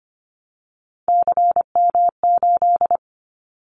Morse code, heard as a keyed tone, about 25 wpm, 700 Hz. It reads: CM8
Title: Callsign Prefixes for Top 100 DXCC - 25wpm